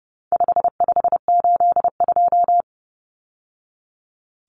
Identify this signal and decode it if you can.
Morse code, heard as a keyed tone, about 30 wpm, 700 Hz. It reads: 5582
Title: Numbers - 4-Digit 30wpm